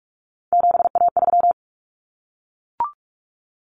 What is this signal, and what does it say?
Morse code, heard as a keyed tone, about 44 wpm, 700 Hz. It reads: 7A3
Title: Common Cut Numbers - 3-Digit 45wpm